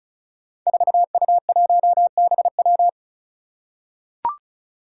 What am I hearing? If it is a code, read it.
Morse code, heard as a keyed tone, about 36 wpm, 700 Hz. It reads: 4U1BW
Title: Five Challenging Letter-Number Combinations 35wpm